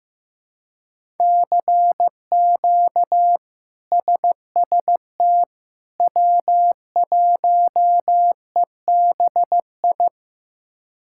Morse code, heard as a keyed tone, about 15 wpm, 700 Hz. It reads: CQ SST W1EBI
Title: K1USN SST QSO Practice - 15wpm